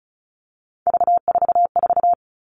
Morse code, heard as a keyed tone, about 36 wpm, 700 Hz. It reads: V44
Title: Three Challenging Letter-Number Combinations 35wpm